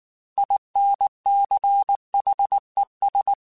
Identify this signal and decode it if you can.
Morse code, heard as a keyed tone, about 19 wpm, 800 Hz. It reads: INCHES
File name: Weather Sets / Weather Sets-mp3s / Weather 3 17 @18